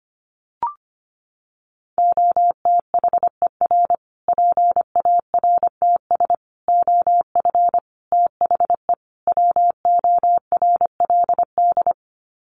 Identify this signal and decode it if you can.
Morse code, heard as a keyed tone, about 25 wpm, 700 Hz. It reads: OTHER PARTS OF THE WORLD
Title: Sets of 5 Words Encore 25wpm